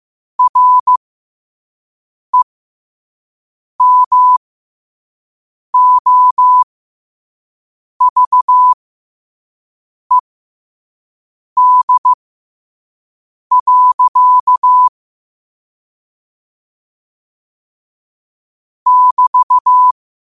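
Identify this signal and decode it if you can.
Morse code, heard as a keyed tone, about 15 wpm, spaced as for 5 wpm, 1000 Hz. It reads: REMOVED. <BT>
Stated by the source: Morse code quote of the day sent at 5 wpm.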